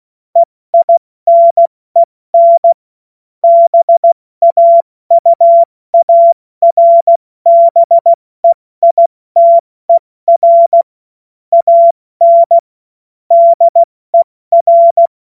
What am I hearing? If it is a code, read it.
Morse code, heard as a keyed tone, about 16 wpm, 680 Hz. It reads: EINEN BAUARBEITER AN DER
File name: Morsekurs / Witze / Witz 06. Januar 2026 70 F80